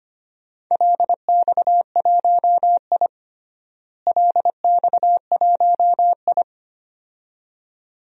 Morse code, heard as a keyed tone, about 25 wpm, 700 Hz. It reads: LX1I LX1I
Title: DXCC-100 Call Signs - 4 Characters - Repeated 2x - 25wpm